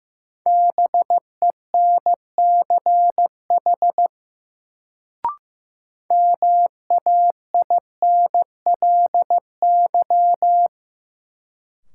Morse code, heard as a keyed tone, about 16 wpm, 700 Hz. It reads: BENCH MAINLY
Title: Sentences from Top 1400 Words - New Word Drill Alt - 15wpm